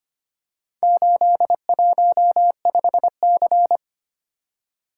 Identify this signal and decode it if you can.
Morse code, heard as a keyed tone, about 25 wpm, 700 Hz. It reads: 815C
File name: practice / 25wpm / Four Letter-Number Combinations - 25wpm